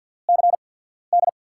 Morse code, heard as a keyed tone, about 49 wpm, spaced as for 14 wpm, 690 Hz. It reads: CD
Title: USA Call Signs - 6 Characters - 8x Character Spacing - 50wpm